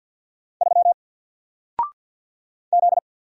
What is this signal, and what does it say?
Morse code, heard as a keyed tone, about 50 wpm, 700 Hz. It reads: 3 Z
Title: Single Letter-Number 50wpm